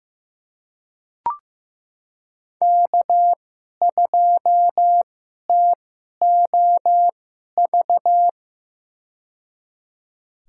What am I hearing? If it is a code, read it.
Morse code, heard as a keyed tone, about 15 wpm, spaced as for 11 wpm, 700 Hz. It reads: K2TOV
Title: USA Call Signs - 5 Characters - 2x Character Spacing - 15wpm